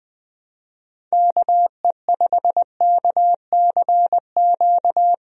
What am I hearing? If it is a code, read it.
Morse code, heard as a keyed tone, about 21 wpm, 700 Hz. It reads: KE5KCQ
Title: USA Call Signs - 6 Characters - Repeated 2x - 20wpm